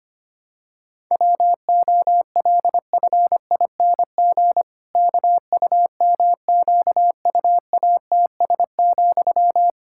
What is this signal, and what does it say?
Morse code, heard as a keyed tone, about 25 wpm, 700 Hz. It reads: WOLFING KUMQUATS,
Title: Pangram Challenge v2 - 25wpm